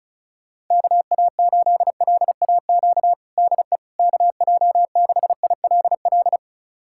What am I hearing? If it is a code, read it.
Morse code, heard as a keyed tone, about 34 wpm, 700 Hz. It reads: KA8LAQ DE KJ6ILL
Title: USA Callsign DE Callsign 35wpm